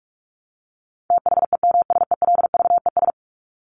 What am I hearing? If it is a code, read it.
Morse code, heard as a keyed tone, about 44 wpm, 700 Hz. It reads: THEMSELVES
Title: Top 751-1000 Words 45wpm